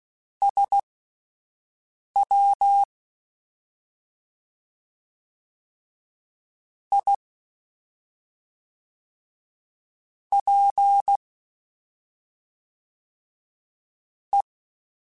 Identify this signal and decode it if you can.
Morse code, heard as a keyed tone, about 16 wpm, spaced as for 5 wpm, 800 Hz. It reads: SW I P E
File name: Misc / Misc-mp3s / Jumps @5